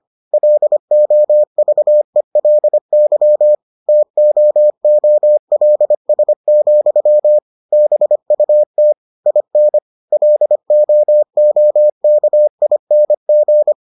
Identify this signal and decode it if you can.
Morse code, heard as a keyed tone, about 25 wpm, 600 Hz. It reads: LOVELY TOOLS, BUT IN LOOKING